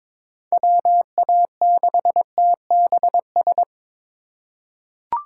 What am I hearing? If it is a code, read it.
Morse code, heard as a keyed tone, about 22 wpm, 700 Hz. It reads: WA6TBS
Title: USA Call Signs - 6 Characters - 6x Character Spacing - 22wpm